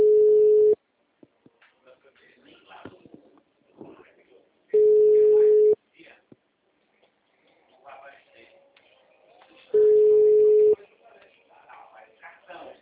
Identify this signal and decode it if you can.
Ringback tone (Europe, 425 Hz), with speech.
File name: out- -2024- - - .6157